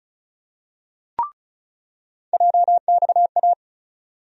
Morse code, heard as a keyed tone, about 36 wpm, 700 Hz. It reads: JXA
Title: USA Call Sign Postfixes - 3 Letters 35wpm